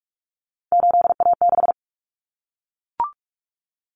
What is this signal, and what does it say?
Morse code, heard as a keyed tone, about 44 wpm, 700 Hz. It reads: 8A6
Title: Common Cut Numbers - 3-Digit 45wpm